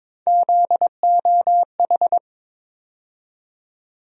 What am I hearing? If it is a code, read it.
Morse code, heard as a keyed tone, about 22 wpm, 700 Hz. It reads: ZOH
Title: USA Call Sign Postfixes - 3 Letters 22wpm